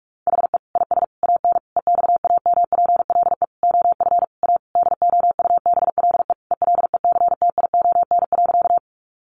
Morse code, heard as a keyed tone, about 44 wpm, 700 Hz. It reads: HE IS AN EXAMPLE OF A DOUBLE ELECTION.
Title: Sentences from Top 1500 Words 45wpm